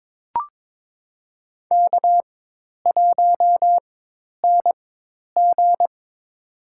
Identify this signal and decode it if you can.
Morse code, heard as a keyed tone, about 22 wpm, spaced as for 10 wpm, 700 Hz. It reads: K1NG
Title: USA Call Signs - 4x Character Spacing - 22wpm